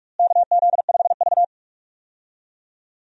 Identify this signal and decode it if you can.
Morse code, heard as a keyed tone, about 44 wpm, 700 Hz. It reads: KZ5V
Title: USA Call Signs - 4 Characters - Repeated 3x - 45wpm